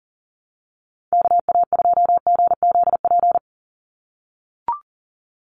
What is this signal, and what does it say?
Morse code, heard as a keyed tone, about 40 wpm, 700 Hz. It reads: KA2GZP
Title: USA Call Signs 40wpm